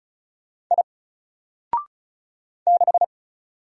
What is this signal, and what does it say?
Morse code, heard as a keyed tone, about 35 wpm, 700 Hz. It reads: I 6
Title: Single Character (Letter:Number:Punctuation:Prosign) 35wpm